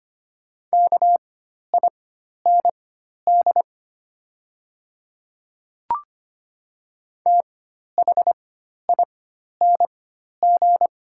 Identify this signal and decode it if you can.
Morse code, heard as a keyed tone, about 25 wpm, spaced as for 12 wpm, 700 Hz. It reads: KIND THING
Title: 4x Character Spacing - Top 500 Words - 25wpm